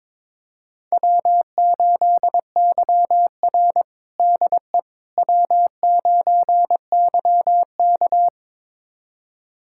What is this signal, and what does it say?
Morse code, heard as a keyed tone, about 22 wpm, 700 Hz. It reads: W8YR DE W9YK
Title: USA Callsign DE Callsign - 4 Characters - 22wpm